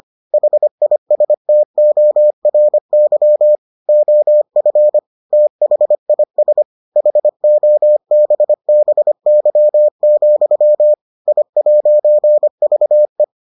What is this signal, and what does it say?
Morse code, heard as a keyed tone, about 25 wpm, 600 Hz. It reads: HISTORY OF THIS HOBBY, I'VE